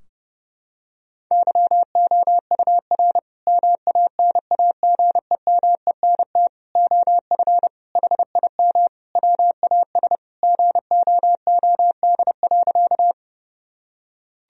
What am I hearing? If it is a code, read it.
Morse code, heard as a keyed tone, about 30 wpm, 700 Hz. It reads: YOUR MANAGEMENT OF HIM WAS GOOD.
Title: Sentences from Top 1000 Words - Review Part 4 - 30wpm